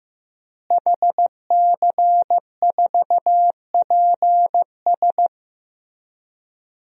Morse code, heard as a keyed tone, about 15 wpm, 700 Hz. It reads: HC4PS
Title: DXCC-100 Call Signs - 5 Characters - Repeated 3x - 15wpm